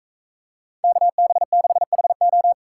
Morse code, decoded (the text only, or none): KB6HO